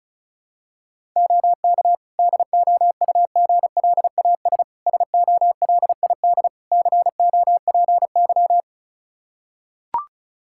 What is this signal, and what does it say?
Morse code, heard as a keyed tone, about 36 wpm, 700 Hz. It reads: OK DOUGLAS SOLID COPY
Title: Practice QSO - Exchange 3 - More Info - 35wpm